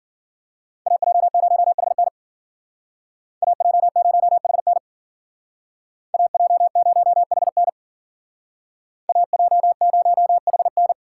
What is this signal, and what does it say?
Morse code, heard as a keyed tone, about 52 wpm, 700 Hz. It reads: AJ0HN AJ0HN AJ0HN AJ0HN